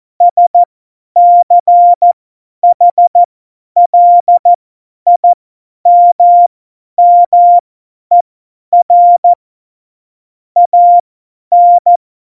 Morse code, heard as a keyed tone, about 14 wpm, spaced as for 10 wpm, 700 Hz. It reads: SCHLIMMER AN
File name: Morsekurs / Witze / Witz 01. Oktober 2019, 50 F70